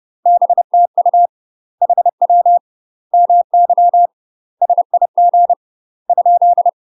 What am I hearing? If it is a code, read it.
Morse code, heard as a keyed tone, about 30 wpm, 700 Hz. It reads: BTU HW MY SIG ?
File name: UK MASTER QSOs TRAINING / 30WPM / UK MASTER QSOs 0020